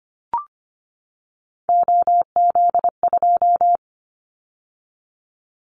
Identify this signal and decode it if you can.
Morse code, heard as a keyed tone, about 25 wpm, 700 Hz. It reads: OZ2
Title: Callsign Prefixes for Top 50 DXCC - 25wpm